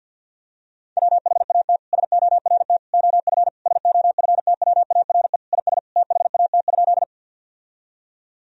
Morse code, heard as a keyed tone, about 52 wpm, 700 Hz. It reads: WHAT SORT OF SOFTWARE IS THAT?
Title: Sentences from Top 1000 Words - Review Part 3 - 50wpm